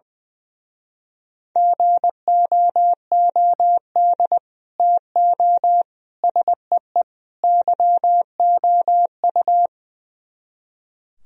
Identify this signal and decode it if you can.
Morse code, heard as a keyed tone, about 20 wpm, 700 Hz. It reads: GOOD TO SEE YOU
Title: Speed-Racing - Sets of 4 Words 20wpm